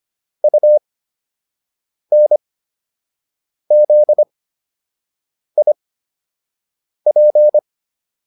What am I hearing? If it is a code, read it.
Morse code, heard as a keyed tone, about 25 wpm, spaced as for 6 wpm, 600 Hz. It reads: UNZIP
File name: UN5 06